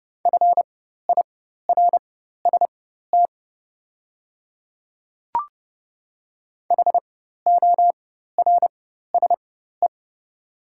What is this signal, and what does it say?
Morse code, heard as a keyed tone, about 30 wpm, spaced as for 14 wpm, 700 Hz. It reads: FIRST HORSE
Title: 4x Character Spacing - Top 500 Words - 30wpm